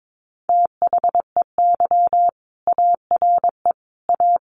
Morse code, heard as a keyed tone, about 22 wpm, 700 Hz. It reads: THEY ARE A
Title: Sentences from Top 1200 Words 22wpm